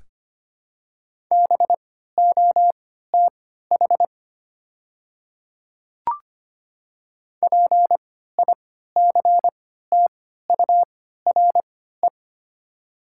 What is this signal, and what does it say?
Morse code, heard as a keyed tone, about 25 wpm, spaced as for 14 wpm, 700 Hz. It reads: BOTH PICTURE
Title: 3x Character Spacing - Top 300 Words - 25wpm